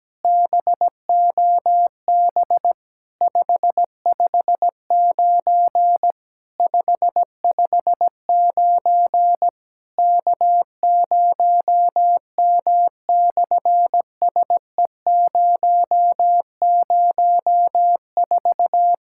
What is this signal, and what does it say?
Morse code, heard as a keyed tone, about 17 wpm, 700 Hz. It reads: BOB 559 559 K0M/SE004
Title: SOTA QSO Practice 17wpm